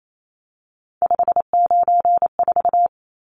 Morse code, heard as a keyed tone, about 28 wpm, 700 Hz. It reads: 594
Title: Numbers - 3-Digit 28wpm